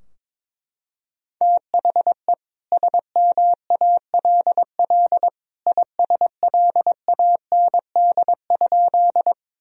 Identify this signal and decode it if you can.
Morse code, heard as a keyed tone, about 22 wpm, 700 Hz. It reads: THE SMALL ISLAND?
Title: Sentences from Top 1000 Words - Review Part 3 - 22wpm